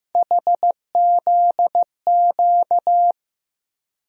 Morse code, heard as a keyed tone, about 16 wpm, 700 Hz. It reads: HZQ
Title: USA Call Sign Postfixes - 3 Letters 15wpm